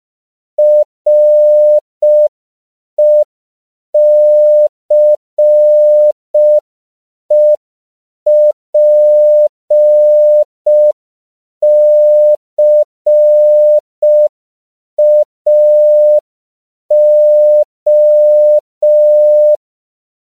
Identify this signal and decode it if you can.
Morse code, heard as a keyed tone, about 5 wpm, 600 Hz. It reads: RECEPCAO
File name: CW / 5PPM